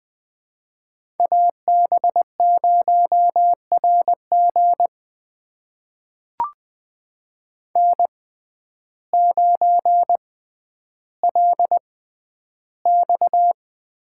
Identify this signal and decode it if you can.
Morse code, heard as a keyed tone, about 20 wpm, spaced as for 11 wpm, 700 Hz. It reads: AB0RG N9LX
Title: USA Call Signs - 5 Characters - 6x Character Spacing - 20wpm